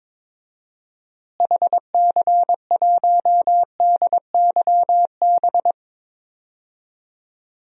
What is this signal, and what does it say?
Morse code, heard as a keyed tone, about 22 wpm, 700 Hz. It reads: HC1DYB
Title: DXCC-100 Call Signs - 6 Characters - Repeated 2x - 22wpm